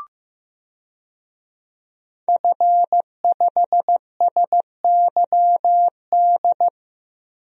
Morse code, heard as a keyed tone, about 15 wpm, 700 Hz. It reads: F5SYD
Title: DXCC-100 Call Signs - 5 Characters - Repeated 2x - 15wpm